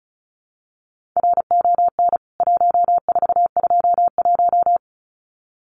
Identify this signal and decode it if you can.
Morse code, heard as a keyed tone, about 34 wpm, 700 Hz. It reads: RON 1421
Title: CWT Exchange 35wpm